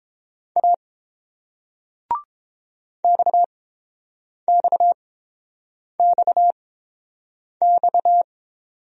Morse code, heard as a keyed tone, about 30 wpm, spaced as for 7 wpm, 700 Hz. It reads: A XXXX